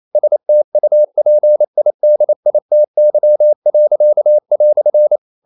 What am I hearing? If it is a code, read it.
Morse code, heard as a keyed tone, about 28 wpm, 600 Hz. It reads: STUPIDITY."
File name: CW / Practice / EnglishQuotations0128